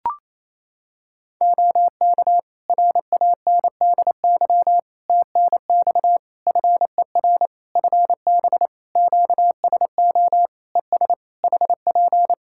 Morse code, heard as a keyed tone, about 27 wpm, 700 Hz. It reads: OK RANDY TNX FER FB QSO ES HP
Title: Practice QSO - Exchange 4 - Ending - 28wpm